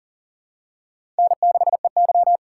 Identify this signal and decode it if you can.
Morse code, heard as a keyed tone, about 40 wpm, 700 Hz. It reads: N6EY
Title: USA Call Signs - 4 Characters - 6x Character Spacing - 40wpm